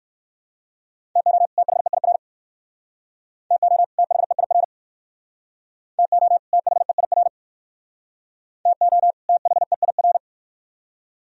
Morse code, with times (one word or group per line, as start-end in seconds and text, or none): 1.2-1.5 TO
1.6-2.2 THEIR
3.5-3.8 TO
4.0-4.6 THEIR
6.0-6.4 TO
6.5-7.3 THEIR
8.7-9.1 TO
9.3-10.2 THEIR